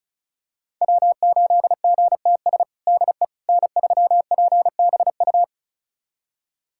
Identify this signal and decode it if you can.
Morse code, heard as a keyed tone, about 34 wpm, 700 Hz. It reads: W8GTS DE N3PBU
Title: USA Callsign DE Callsign - 5 Characters - 35wpm